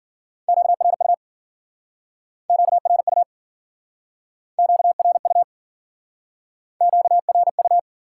Morse code, heard as a keyed tone, about 56 wpm, 700 Hz. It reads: QRU QRU QRU QRU